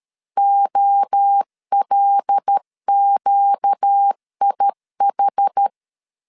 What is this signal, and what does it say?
Morse code, heard as a keyed tone, about 13 wpm, 800 Hz. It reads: OLQIH
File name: LEZIONI / 7 GF9Z / 7 G 12wpm 2055